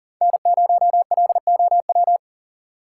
Morse code, heard as a keyed tone, about 37 wpm, 700 Hz. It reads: N0LOW
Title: USA Call Signs - 6x Character Spacing - 40wpm